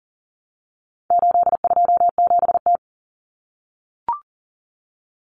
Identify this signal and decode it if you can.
Morse code, heard as a keyed tone, about 40 wpm, 700 Hz. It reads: 827T
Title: Common Cut Numbers - 4-Digit 40wpm